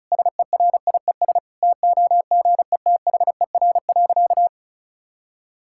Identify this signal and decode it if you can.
Morse code, heard as a keyed tone, about 34 wpm, 700 Hz. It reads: SERIES TOGETHER.
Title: Sentences from Top 1000 Words - Review Part 5 - 35wpm